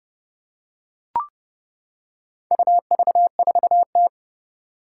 Morse code, heard as a keyed tone, about 30 wpm, 700 Hz. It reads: UV4T
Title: Four Challenging Letter-Number Combinations 30wpm